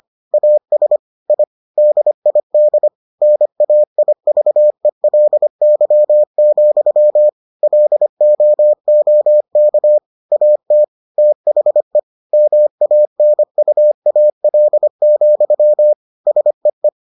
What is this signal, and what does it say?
Morse code, heard as a keyed tone, about 25 wpm, 600 Hz. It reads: AS I DID NAIVELY, LOOK AT THE MANUAL, SEE